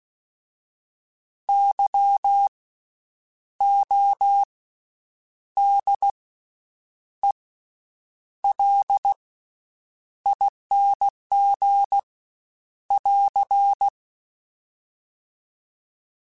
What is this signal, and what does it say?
Morse code, heard as a keyed tone, about 16 wpm, 800 Hz. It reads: Y O D E L ING <AR>
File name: Special INGs-mp3s / Special INGs-mp3s / Slow ing @15